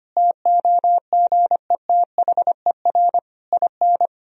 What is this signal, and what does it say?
Morse code, heard as a keyed tone, about 25 wpm, 700 Hz. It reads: TOGETHER IN
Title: Sentences from Top 1000 Words - Review Part 3 - 25wpm